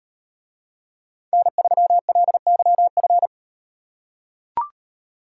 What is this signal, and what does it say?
Morse code, heard as a keyed tone, about 38 wpm, 700 Hz. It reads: N3LYF